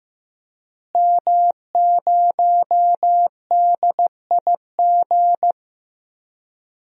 Morse code, heard as a keyed tone, about 15 wpm, 700 Hz. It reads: M0DIG
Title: DXCC-100 Call Signs - 5 Characters - Repeated 3x - 15wpm